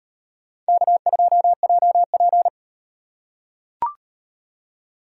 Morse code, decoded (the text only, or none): K2JP